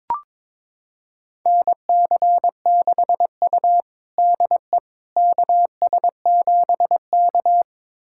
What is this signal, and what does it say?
Morse code, heard as a keyed tone, about 22 wpm, 700 Hz. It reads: NC6U DE KS7K
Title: USA Callsign DE Callsign - 4 Characters - 22wpm